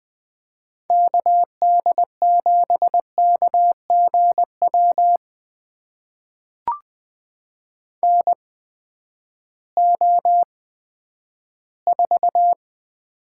Morse code, heard as a keyed tone, about 20 wpm, spaced as for 11 wpm, 700 Hz. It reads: KD7KGW NO4
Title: USA Call Signs - 8x Character Spacing - 20wpm